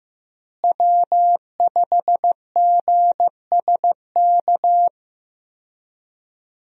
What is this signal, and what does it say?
Morse code, heard as a keyed tone, about 15 wpm, 700 Hz. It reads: W5GSK
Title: USA Call Signs - 5 Characters - Repeated 2x - 15wpm